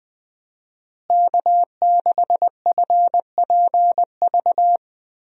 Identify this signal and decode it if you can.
Morse code, heard as a keyed tone, about 20 wpm, 700 Hz. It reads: K6FPV
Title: USA Call Signs - 5 Characters - Repeated 3x - 20wpm